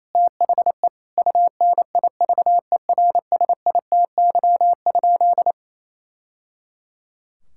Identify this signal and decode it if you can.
Morse code, heard as a keyed tone, about 28 wpm, 700 Hz. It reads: THE UNIVERSITY?
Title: Sentences from Top 1000 Words - Review Part 6 - 28wpm